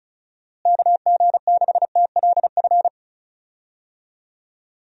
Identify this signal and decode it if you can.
Morse code, heard as a keyed tone, about 36 wpm, 700 Hz. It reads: KG6TLF
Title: USA Call Signs - 6 Characters - Repeated 3x - 35wpm